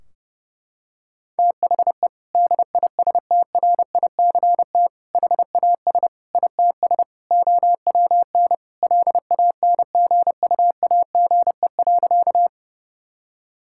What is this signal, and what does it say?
Morse code, heard as a keyed tone, about 32 wpm, 700 Hz. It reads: THE DISTRICT HAS ITS OWN LANGUAGE.
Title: Sentences from Top 1600 Words 30wpm